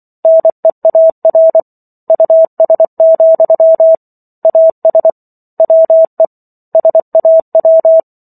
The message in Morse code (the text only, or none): NEAR US, AS WE SAW